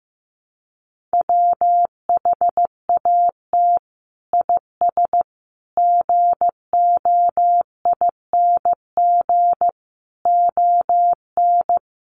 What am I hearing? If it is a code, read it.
Morse code, heard as a keyed tone, about 15 wpm, 700 Hz. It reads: WHAT IS GOING ON
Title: Sets of 5 Words Encore2 15wpm